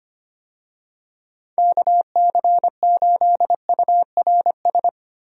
Morse code, heard as a keyed tone, about 25 wpm, 700 Hz. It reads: KC8URS
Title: USA Call Signs - 6 Characters - Repeated 3x - 25wpm